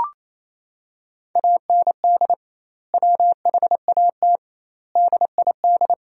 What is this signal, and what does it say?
Morse code, heard as a keyed tone, about 27 wpm, 700 Hz. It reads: AND WHAT DID
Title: 2x Word-Spacing - Sentences from Top 300 Words - 28wpm